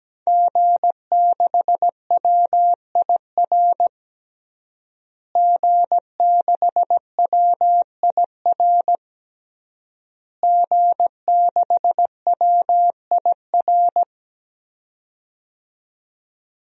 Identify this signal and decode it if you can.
Morse code, heard as a keyed tone, about 17 wpm, 700 Hz. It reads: G6WIR G6WIR G6WIR
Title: DXCC-100 Call Signs - 5 Characters - Repeated 3x - 17wpm